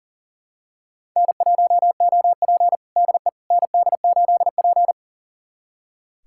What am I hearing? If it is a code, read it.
Morse code, heard as a keyed tone, about 40 wpm, 700 Hz. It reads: N1OP DE ND8P
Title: USA Callsign DE Callsign - 4 Characters - 40wpm